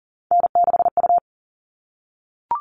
Morse code, heard as a keyed tone, about 40 wpm, 700 Hz. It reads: N6U
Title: All Cut Numbers - 3-Digit 40wpm